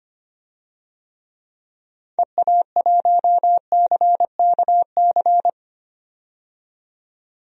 Morse code, heard as a keyed tone, about 25 wpm, 700 Hz. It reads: EA1CKC
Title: DXCC-100 Call Signs - 6 Characters - 25wpm